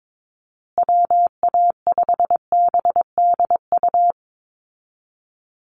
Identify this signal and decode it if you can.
Morse code, heard as a keyed tone, about 22 wpm, 700 Hz. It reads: WA5BDU
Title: Call Signs Repeated 2x - 22wpm